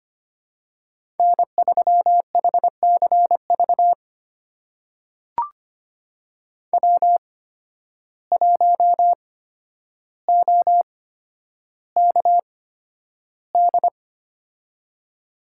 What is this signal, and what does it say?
Morse code, heard as a keyed tone, about 25 wpm, spaced as for 11 wpm, 700 Hz. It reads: N3HCV W1OKD
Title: USA Call Signs - 5 Characters - 8x Character Spacing - 25wpm